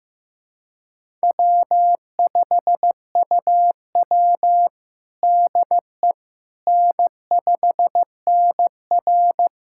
Morse code, heard as a keyed tone, about 15 wpm, 700 Hz. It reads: W5UW DE N5NR
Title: USA Callsign DE Callsign - 4 Characters - 15wpm